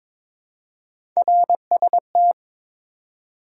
Morse code, heard as a keyed tone, about 22 wpm, 700 Hz. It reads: RST
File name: Top 90 QSO Elements - 2x Character Spacing - 22wpm